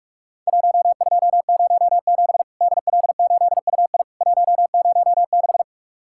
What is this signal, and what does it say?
Morse code, heard as a keyed tone, about 44 wpm, 700 Hz. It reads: 1207 DL8UI 106
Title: Worked All Europe DX Contest - QTC Exchange - 45wpm